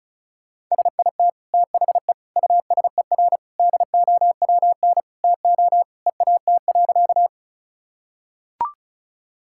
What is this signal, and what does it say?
Morse code, heard as a keyed tone, about 36 wpm, 700 Hz. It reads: SIT THE USER DOWN TO EAT.
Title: Sentences from Top 1000 Words - Review Part 5 - 35wpm